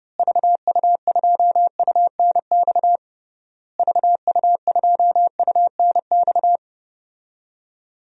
Morse code, heard as a keyed tone, about 32 wpm, 700 Hz. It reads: VU2UNX VU2UNX
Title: DXCC-100 Call Signs - 6 Characters - Repeated 3x - 30wpm